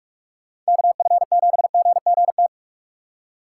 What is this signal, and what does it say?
Morse code, heard as a keyed tone, about 59 wpm, spaced as for 51 wpm, 700 Hz. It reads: KF7GGT
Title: Speed Racing - USA 2x3 Call Signs 45wpm